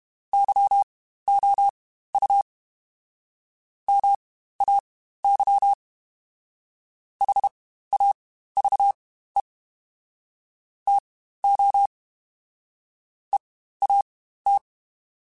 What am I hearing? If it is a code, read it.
Morse code, heard as a keyed tone, about 32 wpm, spaced as for 15 wpm, 800 Hz. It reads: YOU MAY HAVE TO EAT